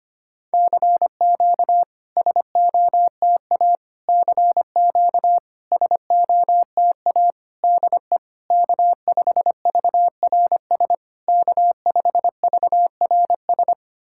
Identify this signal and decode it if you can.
Morse code, heard as a keyed tone, about 25 wpm, 690 Hz. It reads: CQ SOTA CQ SOTA DE K5VRS K5VRS
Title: SOTA QSO Practice 25wpm